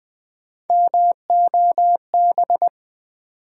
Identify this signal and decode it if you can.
Morse code, heard as a keyed tone, about 20 wpm, 700 Hz. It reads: MOB
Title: Build A Word - 1 Letter At A Time - 3-Letter Words - 20wpm